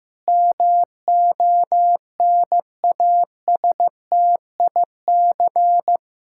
Morse code, heard as a keyed tone, about 15 wpm, 700 Hz. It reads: MONASTIC
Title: Pangram Challenge v2 - 15wpm